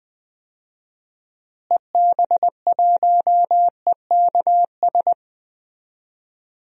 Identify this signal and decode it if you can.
Morse code, heard as a keyed tone, about 20 wpm, 700 Hz. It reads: EB1EKS
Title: DXCC-100 Call Signs - 6 Characters - Repeated 2x - 20wpm